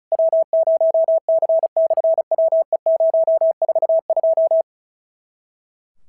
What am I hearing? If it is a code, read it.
Morse code, heard as a keyed tone, about 36 wpm, 650 Hz. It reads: W0C/WE042
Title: SOTA QSO Practice 35wpm